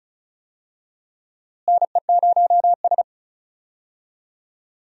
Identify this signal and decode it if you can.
Morse code, heard as a keyed tone, about 36 wpm, 700 Hz. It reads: NE0S
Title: USA Call Signs - 4 Characters - Repeated 3x - 35wpm